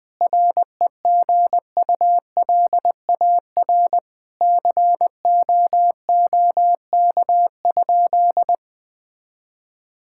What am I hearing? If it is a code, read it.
Morse code, heard as a keyed tone, about 20 wpm, 700 Hz. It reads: REGULAR COOK?
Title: Sentences from Top 1000 Words - Review Part 6 - 20wpm